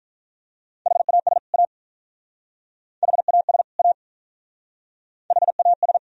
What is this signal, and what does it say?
Morse code, heard as a keyed tone, about 44 wpm, 700 Hz. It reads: HAS A HAS A HAS